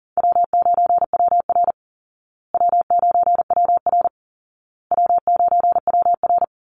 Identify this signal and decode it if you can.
Morse code, heard as a keyed tone, about 37 wpm, 700 Hz. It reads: W9WR W9WR W9WR
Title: Call Signs Repeated 2x - 40wpm